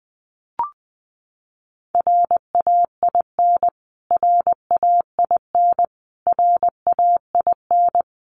Morse code, heard as a keyed tone, about 20 wpm, 700 Hz. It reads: RAIN RAIN RAIN
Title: Top 500 Words Repeated 20wpm